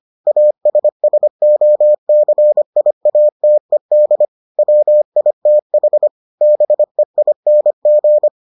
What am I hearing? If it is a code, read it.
Morse code, heard as a keyed tone, about 25 wpm, 600 Hz. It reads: ASSOCIATED WITH BEING